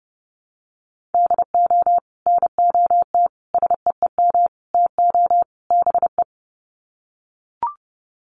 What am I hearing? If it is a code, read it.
Morse code, heard as a keyed tone, about 30 wpm, 700 Hz. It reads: DO NOT SEEM TO BE
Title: Sets of 5 Words Encore2 30wpm